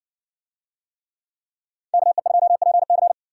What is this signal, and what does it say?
Morse code, heard as a keyed tone, about 60 wpm, 700 Hz. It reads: KE2PC